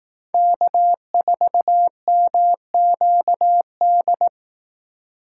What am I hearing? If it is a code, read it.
Morse code, heard as a keyed tone, about 18 wpm, 700 Hz. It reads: K4MQD
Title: Speed Racing - USA 1x3 Call Signs 15wpm